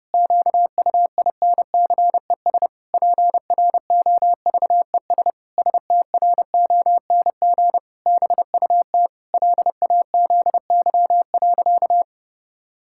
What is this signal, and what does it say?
Morse code, heard as a keyed tone, about 30 wpm, 700 Hz. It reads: QUINCES PROVES STRONG BUT LAZY.
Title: Pangram Challenge v2 - 30wpm